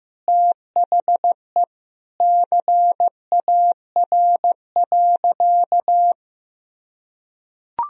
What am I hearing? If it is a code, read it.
Morse code, heard as a keyed tone, about 15 wpm, 700 Hz. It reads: THE CAR.
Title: Sentences from Top 1000 Words - Review Part 1 - 15wpm